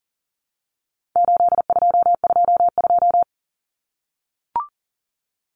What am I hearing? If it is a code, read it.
Morse code, heard as a keyed tone, about 40 wpm, 700 Hz. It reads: 8222
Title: Common Cut Numbers - 4-Digit 40wpm